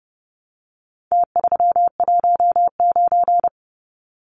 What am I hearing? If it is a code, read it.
Morse code, heard as a keyed tone, about 30 wpm, 700 Hz. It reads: T319
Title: Numbers - Contest Style - 4-Digit 30wpm